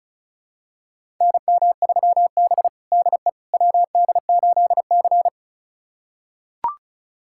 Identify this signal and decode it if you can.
Morse code, heard as a keyed tone, about 36 wpm, 700 Hz. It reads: NM3B DE WD8C
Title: USA Callsign DE Callsign - 4 Characters - 35wpm